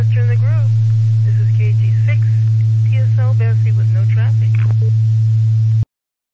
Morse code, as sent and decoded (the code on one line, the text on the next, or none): none